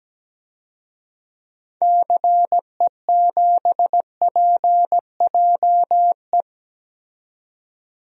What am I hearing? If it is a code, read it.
Morse code, heard as a keyed tone, about 17 wpm, 700 Hz. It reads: CE7PJE
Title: DXCC-100 Call Signs - 6 Characters - 17wpm